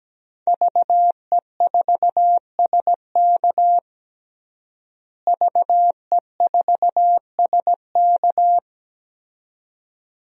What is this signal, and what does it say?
Morse code, heard as a keyed tone, about 17 wpm, 700 Hz. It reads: VE4SK VE4SK
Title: DXCC-100 Call Signs - 5 Characters - Repeated 2x - 17wpm